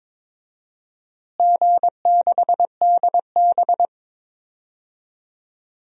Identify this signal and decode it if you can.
Morse code, heard as a keyed tone, about 22 wpm, 700 Hz. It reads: G6DB
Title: DXCC-100 Call Signs - 4 Characters - Repeated 2x - 22wpm